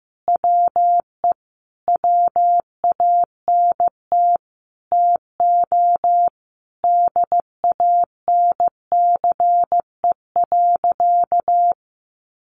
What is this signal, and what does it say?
Morse code, heard as a keyed tone, about 15 wpm, 700 Hz. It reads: WE WANT TO DANCE.
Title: Sentences from Top 700 Words 15wpm